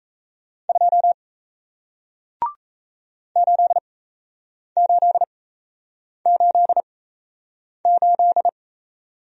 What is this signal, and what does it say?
Morse code, heard as a keyed tone, about 37 wpm, spaced as for 8 wpm, 700 Hz. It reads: 2 8888